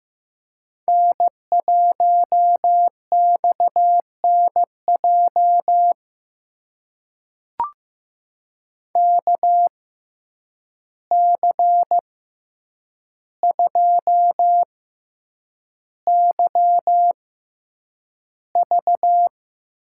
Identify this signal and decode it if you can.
Morse code, heard as a keyed tone, about 15 wpm, spaced as for 8 wpm, 700 Hz. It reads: N1XNJ KC2YV
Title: USA Call Signs - 6x Character Spacing - 15wpm